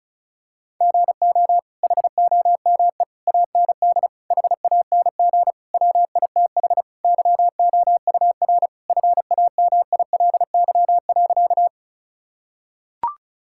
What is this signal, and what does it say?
Morse code, heard as a keyed tone, about 36 wpm, 700 Hz. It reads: GO HOME AND HANG WITH YOUR FAMILY.
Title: Sentences from Top 1600 Words 35wpm